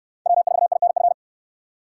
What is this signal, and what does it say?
Morse code, heard as a keyed tone, about 69 wpm, 700 Hz. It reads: W4INF